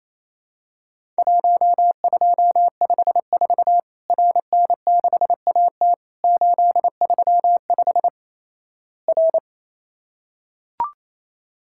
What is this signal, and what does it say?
Morse code, heard as a keyed tone, about 28 wpm, 700 Hz. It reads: 1254 RN6AT 835 R
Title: Worked All Europe DX Contest - QTC Exchange - 28wpm